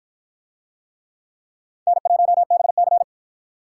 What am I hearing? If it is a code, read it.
Morse code, heard as a keyed tone, about 47 wpm, 700 Hz. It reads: N1BC